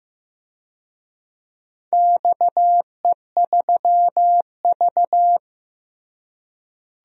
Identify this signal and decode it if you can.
Morse code, heard as a keyed tone, about 16 wpm, 700 Hz. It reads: XE3V
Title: DXCC-100 Call Signs - 4 Characters - Repeated 2x - 15wpm